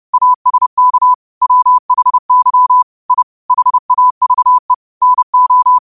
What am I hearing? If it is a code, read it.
Morse code, heard as a keyed tone, about 30 wpm, 1000 Hz. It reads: ASK WHY I HAVE NO
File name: qotdfiles / 30wpm